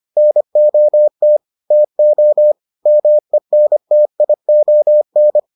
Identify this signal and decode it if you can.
Morse code, heard as a keyed tone, about 25 wpm, 600 Hz. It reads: NOT TO MENTION